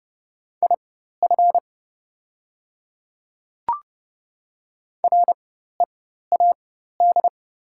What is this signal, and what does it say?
Morse code, heard as a keyed tone, about 30 wpm, spaced as for 14 wpm, 700 Hz. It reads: IF READ
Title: 4x Character Spacing - Top 300 Words - 30wpm